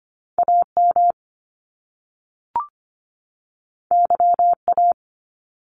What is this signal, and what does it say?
Morse code, heard as a keyed tone, about 25 wpm, 700 Hz. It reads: AM YA
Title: Top Two Letter Combinations in Words 25wpm